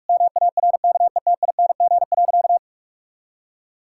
Morse code, heard as a keyed tone, about 43 wpm, 700 Hz. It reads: MARKETING.
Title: Sentences from Top 1000 Words - Review Part 5 - 45wpm